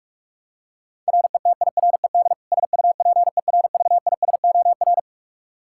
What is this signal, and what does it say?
Morse code, heard as a keyed tone, about 46 wpm, 700 Hz. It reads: RETIRED SUPERVISOR
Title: Practice QSO - Exchange 3 - More Info - 45wpm